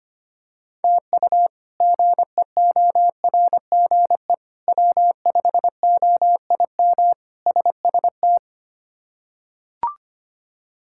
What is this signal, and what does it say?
Morse code, heard as a keyed tone, about 25 wpm, 700 Hz. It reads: TU GEORGE W5OIM SST
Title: K1USN SST QSO Practice - 25wpm